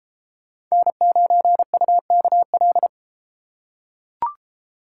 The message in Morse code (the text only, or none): N9UKL